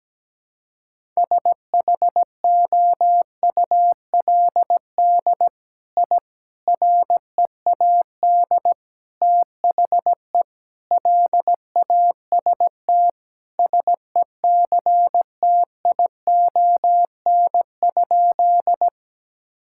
Morse code, heard as a keyed tone, about 17 wpm, 700 Hz. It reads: SHOULD I READ THE LAST SECTION?
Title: Sentences from Top 1000 Words - Review Part 5 - 17wpm